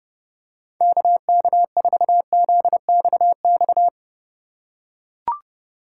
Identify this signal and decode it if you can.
Morse code, heard as a keyed tone, about 29 wpm, 700 Hz. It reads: KK4ZXX
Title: USA Call Signs - 6 Characters - 3x Character Spacing - 30wpm